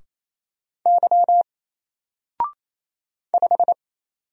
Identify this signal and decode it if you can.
Morse code, heard as a keyed tone, about 27 wpm, 700 Hz. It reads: Y 5
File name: Single-Letter-Number-28wpm